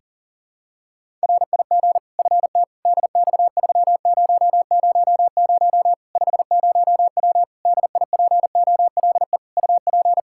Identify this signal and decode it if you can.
Morse code, heard as a keyed tone, about 37 wpm, 700 Hz. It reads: RIG FT DX3000 50W DIPOLE UP
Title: Practice QSO - Exchange 2 - Info - 40wpm